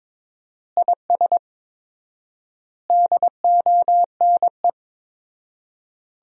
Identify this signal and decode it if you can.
Morse code, heard as a keyed tone, about 22 wpm, 700 Hz. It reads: IS DONE
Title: 4x Word-Spacing - Sentences from Top 500 Words - 22wpm